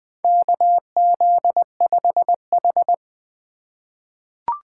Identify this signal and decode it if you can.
Morse code, heard as a keyed tone, about 20 wpm, 700 Hz. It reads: KZ5H
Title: USA Call Signs - 4 Characters - 4x Character Spacing - 20wpm